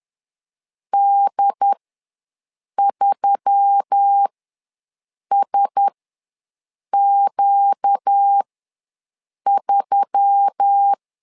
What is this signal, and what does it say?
Morse code, heard as a keyed tone, about 10 wpm, spaced as for 6 wpm, 800 Hz. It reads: D3SQ3
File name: LEZIONI / 4 Q1S3 ws 10wpm 2437